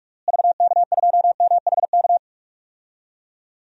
Morse code, heard as a keyed tone, about 45 wpm, 700 Hz. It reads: VK2MHK
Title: DXCC-100 Call Signs - 6 Characters - Repeated 2x - 45wpm